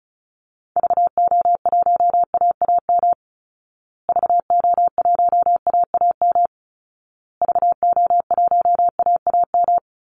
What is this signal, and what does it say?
Morse code, heard as a keyed tone, about 34 wpm, 700 Hz. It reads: VO1AAM VO1AAM VO1AAM
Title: Call Signs Repeated 2x - 35wpm